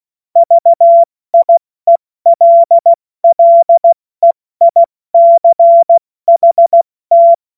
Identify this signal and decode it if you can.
Morse code, heard as a keyed tone, about 16 wpm, 680 Hz. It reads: VIELLEICHT
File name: Witz-22-Juli-2025-70-F80